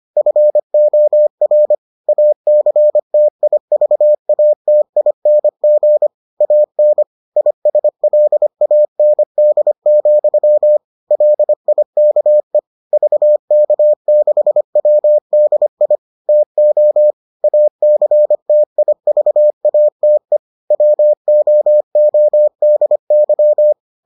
Morse code, heard as a keyed tone, about 25 wpm, 600 Hz. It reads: FOR ACTIVATING AN ISLAND, LIKE VK6WDI TO ACTIVATE WOODY